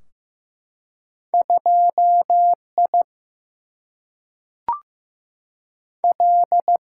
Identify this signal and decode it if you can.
Morse code, heard as a keyed tone, about 15 wpm, 700 Hz. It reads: 2I L
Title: Single-Letter Single-Number 15wpm